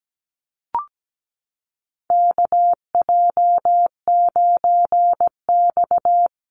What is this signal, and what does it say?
Morse code, heard as a keyed tone, about 17 wpm, 700 Hz. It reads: KJ9X
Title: NA SOTA Chaser Callsigns 17wpm